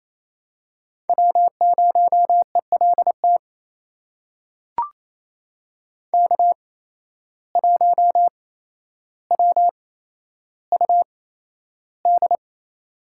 Morse code, heard as a keyed tone, about 28 wpm, spaced as for 12 wpm, 700 Hz. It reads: W0ELT K1WUD
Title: USA Call Signs - 5 Characters - 8x Character Spacing - 28wpm